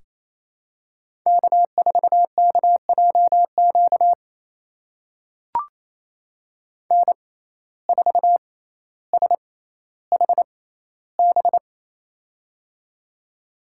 Morse code, heard as a keyed tone, about 27 wpm, spaced as for 14 wpm, 700 Hz. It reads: K4KJQ N4SHB
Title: USA Call Signs - 5 Characters - 6x Character Spacing - 28wpm